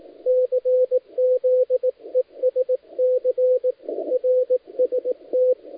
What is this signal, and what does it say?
Morse code, heard as a keyed tone, about 18 wpm, 500 Hz, with radio noise. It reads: CZESC RST
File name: dlazaawansowanych / nagraniazpasma / titawaNETcontrol